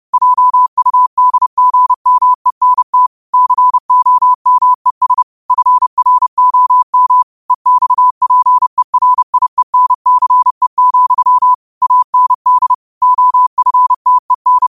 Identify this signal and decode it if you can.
Morse code, heard as a keyed tone, about 30 wpm, 1000 Hz. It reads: JUDGMENT COMES FROM EXPERIENCE, AND OFTEN
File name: qotdfiles / 30wpm